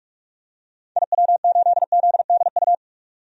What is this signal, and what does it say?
Morse code, heard as a keyed tone, about 42 wpm, 700 Hz. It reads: IW8ZDU
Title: DXCC-100 Call Signs - 6 Characters - 45wpm